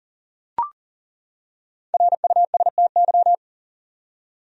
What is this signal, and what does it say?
Morse code, heard as a keyed tone, about 40 wpm, 700 Hz. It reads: RUSTY
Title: USA Common Names - Max 5 Letters - 40wpm